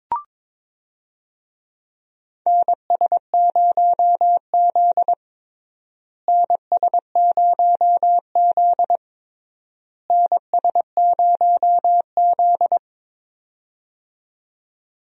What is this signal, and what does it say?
Morse code, heard as a keyed tone, about 21 wpm, 700 Hz. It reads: NS0Z NS0Z NS0Z
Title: USA Call Signs - 4 Characters - Repeated 3x - 22wpm